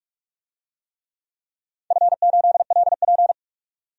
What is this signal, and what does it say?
Morse code, heard as a keyed tone, about 46 wpm, 700 Hz. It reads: F8LP
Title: DXCC-100 Call Signs - 4 Characters - 45wpm